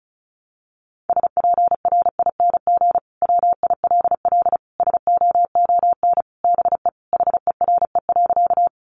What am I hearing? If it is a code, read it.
Morse code, heard as a keyed tone, about 34 wpm, 700 Hz. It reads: SPRING WILL SOON BE HERE.
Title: Sentences from Top 1200 Words 35wpm